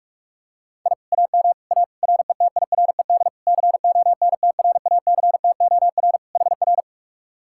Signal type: Morse code

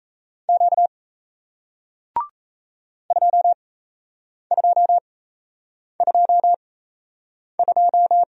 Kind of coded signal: Morse code

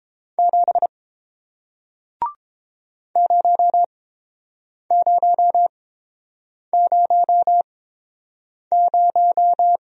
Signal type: Morse code

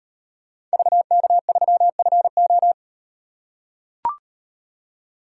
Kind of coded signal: Morse code

